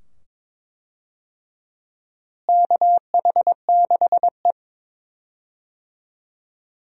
Morse code, heard as a keyed tone, about 22 wpm, 700 Hz. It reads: KH6E
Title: DXCC-100 Call Signs - 4 Characters - Repeated 2x - 22wpm